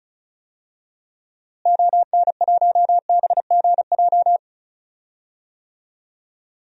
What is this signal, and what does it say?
Morse code, heard as a keyed tone, about 36 wpm, 700 Hz. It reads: ON1BGJ
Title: DXCC-100 Call Signs - 6 Characters - Repeated 2x - 35wpm